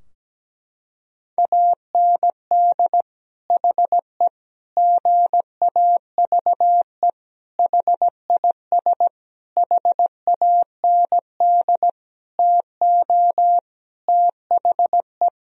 Morse code, heard as a keyed tone, about 17 wpm, 700 Hz. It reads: AND HE GAVE HIS HAND TO THE
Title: Sentences from Top 1000 Words - Review Part 4 - 17wpm